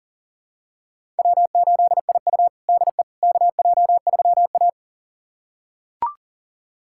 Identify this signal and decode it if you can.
Morse code, heard as a keyed tone, about 40 wpm, 700 Hz. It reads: W8IU DE KJ3A
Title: USA Callsign DE Callsign - 4 Characters - 40wpm